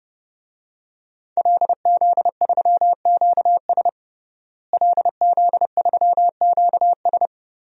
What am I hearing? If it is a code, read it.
Morse code, heard as a keyed tone, about 30 wpm, 700 Hz. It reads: LZ3QS LZ3QS
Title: DXCC-100 Call Signs - 5 Characters - Repeated 3x - 30wpm